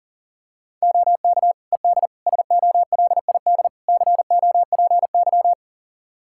Morse code, heard as a keyed tone, about 37 wpm, 700 Hz. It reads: OK ED SOLID COPY
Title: Practice QSO - Exchange 3 - More Info - 40wpm